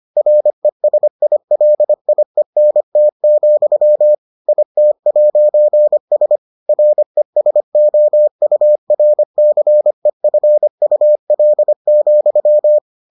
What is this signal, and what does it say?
Morse code, heard as a keyed tone, about 25 wpm, 600 Hz. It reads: RESILIENT, IT'S RESOURCEFUL,